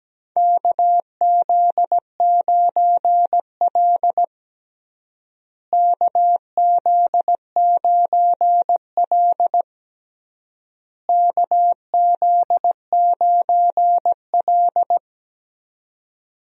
Morse code, heard as a keyed tone, about 17 wpm, 700 Hz. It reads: KZ9L KZ9L KZ9L
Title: USA Call Signs - 4 Characters - Repeated 3x - 17wpm